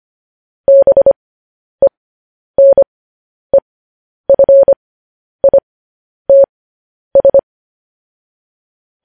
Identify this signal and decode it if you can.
Morse code, heard as a keyed tone, about 25 wpm, spaced as for 10 wpm, 550 Hz. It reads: BENEFITS
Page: news-headlines-in-morse-code-at-10-wpm